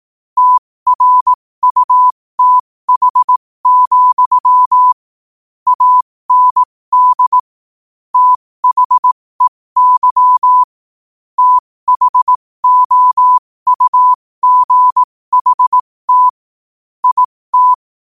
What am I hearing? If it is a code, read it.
Morse code, heard as a keyed tone, about 18 wpm, spaced as for 15 wpm, 1000 Hz. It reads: TRUTH, AND THEY THOUGHT IT
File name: qotdfiles / 15wpm